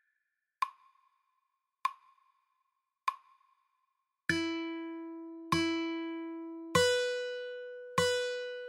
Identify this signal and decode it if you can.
There is no code.